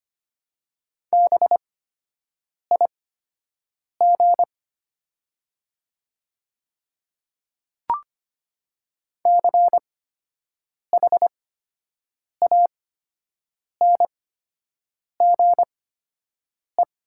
Morse code, heard as a keyed tone, about 25 wpm, spaced as for 7 wpm, 700 Hz. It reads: BIG CHANGE
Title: 8x Character Spacing - Top 200 Words - 25wpm